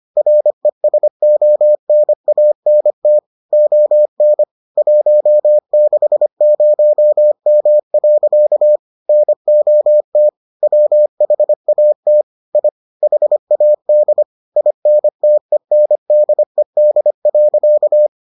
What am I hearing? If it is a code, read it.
Morse code, heard as a keyed tone, about 25 wpm, 600 Hz. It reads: RESONANT ON 160M. NOT WHAT I HAD INTENDED.